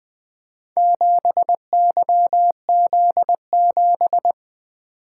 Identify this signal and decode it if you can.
Morse code, heard as a keyed tone, about 20 wpm, 700 Hz. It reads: 7YZ7
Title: Four Letter-Number Combinations - 20wpm